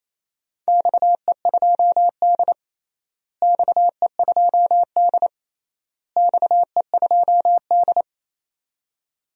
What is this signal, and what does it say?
Morse code, heard as a keyed tone, about 28 wpm, 700 Hz. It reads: XE2D XE2D XE2D
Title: DXCC-100 Call Signs - 4 Characters - Repeated 3x - 28wpm